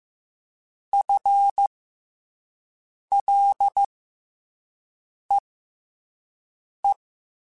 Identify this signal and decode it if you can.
Morse code, heard as a keyed tone, about 15 wpm, spaced as for 5 wpm, 800 Hz. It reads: FLEE
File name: FL / FL mp3s / FL Prefixes 3x2 Twice @5